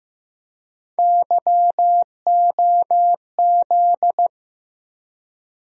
Morse code, heard as a keyed tone, about 15 wpm, 700 Hz. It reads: YOZ
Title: USA Call Sign Postfixes - 3 Letters 15wpm